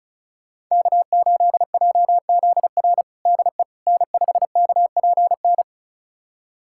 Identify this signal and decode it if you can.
Morse code, heard as a keyed tone, about 34 wpm, 700 Hz. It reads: K8JZR DE N5KPN
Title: USA Callsign DE Callsign - 5 Characters - 35wpm